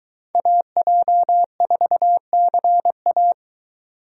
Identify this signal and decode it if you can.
Morse code, heard as a keyed tone, about 22 wpm, 700 Hz. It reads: AJ4CA